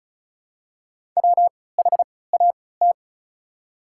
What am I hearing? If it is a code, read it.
Morse code, heard as a keyed tone, about 36 wpm, spaced as for 20 wpm, 700 Hz. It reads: WHAT
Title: 3x Character Spacing - Top 100 Words - 35wpm